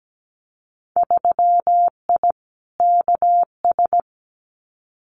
Morse code, heard as a keyed tone, about 17 wpm, 700 Hz. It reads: 3I KS
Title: Winter Field Day Exchanges 17wpm